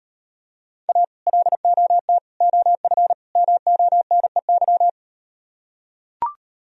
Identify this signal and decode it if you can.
Morse code, heard as a keyed tone, about 38 wpm, 700 Hz. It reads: A LOT OF MONEY